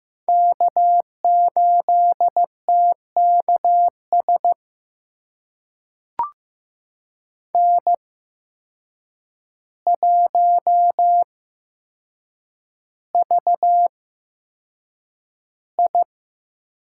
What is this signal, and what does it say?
Morse code, heard as a keyed tone, about 15 wpm, spaced as for 7 wpm, 700 Hz. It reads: K8TKS N1VI
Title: USA Call Signs - 5 Characters - 8x Character Spacing - 15wpm